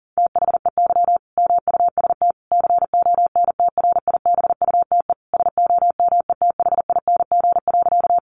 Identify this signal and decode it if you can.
Morse code, heard as a keyed tone, about 40 wpm, 700 Hz. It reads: THEY MUST CONTRIBUTE SOMETHING.
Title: Sentences from Top 1400 Words 40wpm